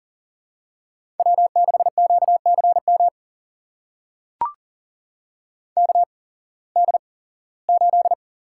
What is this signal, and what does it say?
Morse code, heard as a keyed tone, about 40 wpm, spaced as for 21 wpm, 700 Hz. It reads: W6QCM KD8
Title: USA Call Signs - 5 Characters - 8x Character Spacing - 40wpm